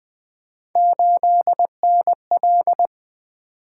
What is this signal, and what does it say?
Morse code, heard as a keyed tone, about 20 wpm, 700 Hz. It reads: 8NL
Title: Three Letter-Number Combinations - 20wpm